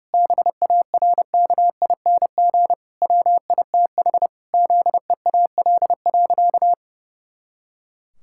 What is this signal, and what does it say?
Morse code, heard as a keyed tone, about 32 wpm, 700 Hz. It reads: BARKING WITH ZEAL.
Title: Pangram Challenge v2 - 30wpm